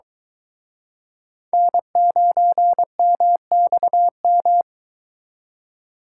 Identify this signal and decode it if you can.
Morse code, heard as a keyed tone, about 23 wpm, 700 Hz. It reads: N9MXM